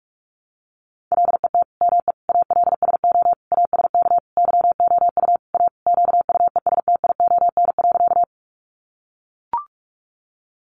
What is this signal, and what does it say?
Morse code, heard as a keyed tone, about 44 wpm, 700 Hz. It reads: LET ME ALSO ASK YOU A QUESTION.
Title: Sentences from Top 400 Words 45wpm